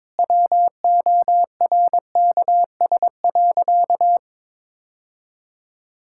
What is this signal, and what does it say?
Morse code, heard as a keyed tone, about 22 wpm, 700 Hz. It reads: WORKS.
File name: practice / 22wpm / Sentences with 3-Word Phrases - 22wpm